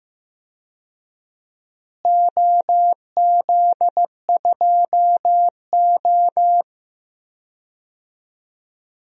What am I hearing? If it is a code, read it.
Morse code, heard as a keyed tone, about 15 wpm, 700 Hz. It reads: OZ2O
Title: DXCC-100 Call Signs - 4 Characters - Repeated 2x - 15wpm